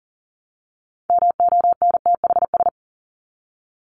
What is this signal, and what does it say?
Morse code, heard as a keyed tone, about 40 wpm, 700 Hz. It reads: MONTHS
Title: ES:S Suffix - Top 100 Words with Base Words 40wpm